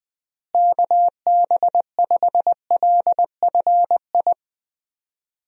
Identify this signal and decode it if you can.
Morse code, heard as a keyed tone, about 20 wpm, 700 Hz. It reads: KB5LFI
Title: USA Call Signs - 6 Characters - Repeated 3x - 20wpm